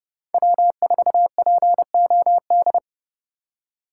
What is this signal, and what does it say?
Morse code, heard as a keyed tone, about 35 wpm, 700 Hz. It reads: W4POD
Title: Speed Racing - USA 1x3 Call Signs 30wpm